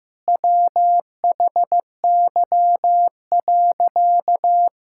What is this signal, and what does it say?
Morse code, heard as a keyed tone, about 16 wpm, 700 Hz. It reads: WHY.
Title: Sentences from Top 1000 Words - Review Part 6 - 15wpm